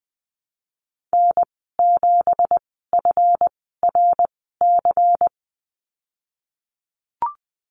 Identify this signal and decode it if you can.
Morse code, heard as a keyed tone, about 20 wpm, spaced as for 15 wpm, 700 Hz. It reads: N7FRC
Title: 167 Lesson 27g - Number 7 - Call Signs 20wpm